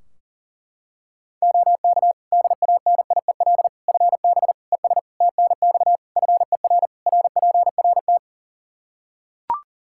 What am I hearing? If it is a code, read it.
Morse code, heard as a keyed tone, about 40 wpm, 700 Hz. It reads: OK DANIEL FB ES TNX FER RPRT
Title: Practice QSO - Exchange 2 - Info - 40wpm